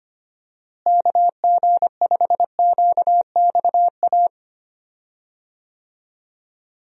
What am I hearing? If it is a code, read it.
Morse code, heard as a keyed tone, about 25 wpm, 700 Hz. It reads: KG5QXA
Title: USA Call Signs - 6 Characters - Repeated 3x - 25wpm